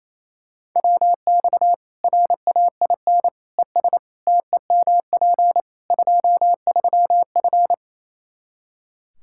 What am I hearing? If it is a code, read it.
Morse code, heard as a keyed tone, about 28 wpm, 700 Hz. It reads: WX RAIN ES TEMP 23F
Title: Practice QSO - Exchange 2 - Info - 28wpm